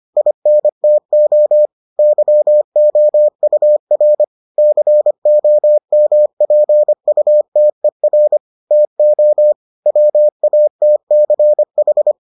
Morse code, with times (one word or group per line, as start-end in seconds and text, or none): 0.2-1.7 INTO
2.0-4.2 YOUR
4.6-8.4 COMPUTER
8.7-9.5 TO
9.9-12.1 WATCH